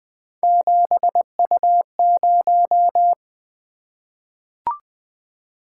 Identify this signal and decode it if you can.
Morse code, heard as a keyed tone, about 20 wpm, 700 Hz. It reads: 7U0
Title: Three Letter-Number Combinations - 20wpm